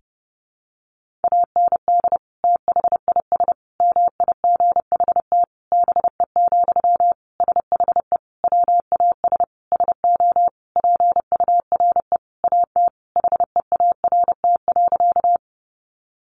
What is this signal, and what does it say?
Morse code, heard as a keyed tone, about 28 wpm, 700 Hz. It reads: AND THIS MIGHT BE, SHE WAS SO PURE AT HEART.
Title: Sentences from Top 1500 Words 30wpm